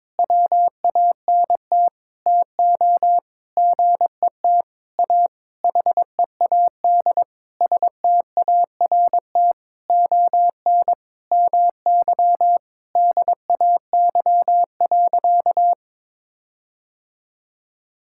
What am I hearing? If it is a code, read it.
Morse code, heard as a keyed tone, about 22 wpm, 700 Hz. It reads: WANT TO GET A HEAD START ON MY DAY.
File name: Sentences with 9-Word Phrases - 22wpm